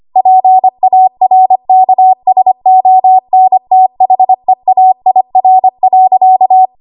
Morse code, heard as a keyed tone, about 25 wpm, 750 Hz. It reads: PARKSONTHEAIR.
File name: Archive / 25 WPM / 25WPM